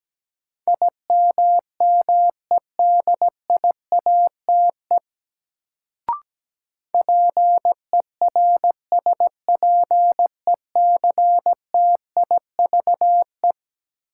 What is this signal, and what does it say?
Morse code, heard as a keyed tone, about 17 wpm, 700 Hz. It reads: IMMEDIATE PERSPECTIVE
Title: Sentences from Top 1200 Words - New Word Drill - 17wpm